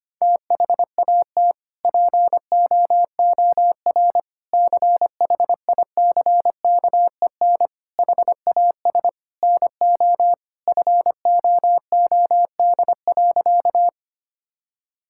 Morse code, heard as a keyed tone, about 25 wpm, 700 Hz. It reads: THAT POOR CHICKEN HAS NO FOOD.
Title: Sentences from Top 1000 Words - Review Part 5 - 25wpm